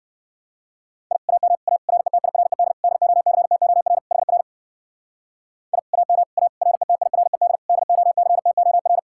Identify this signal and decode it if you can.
Morse code, heard as a keyed tone, about 56 wpm, 700 Hz. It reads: I AM A RETIRED DOCTOR HR I AM A RETIRED DOCTOR
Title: Speed-Racing Common QSO Phrases - 45wpm